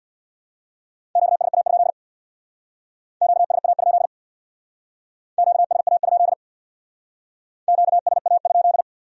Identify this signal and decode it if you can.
Morse code, heard as a keyed tone, about 61 wpm, 700 Hz. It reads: QSA? QSA? QSA? QSA?
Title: Speed-Racing Q-Codes - 50wpm